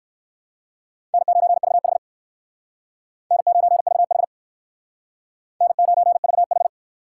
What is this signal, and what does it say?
Morse code, heard as a keyed tone, about 60 wpm, 700 Hz. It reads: N9VH N9VH N9VH